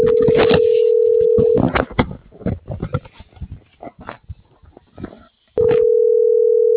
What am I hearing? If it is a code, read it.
Ringback tone (North America), 440 + 480 Hz.